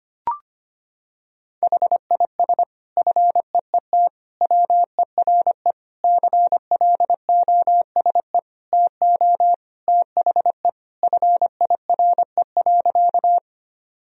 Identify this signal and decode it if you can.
Morse code, heard as a keyed tone, about 25 wpm, 700 Hz. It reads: HIS FEET WERE CLOSE TO THE FIRE.
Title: Sentences from Top 1000 Words - Review Part 1 - 25wpm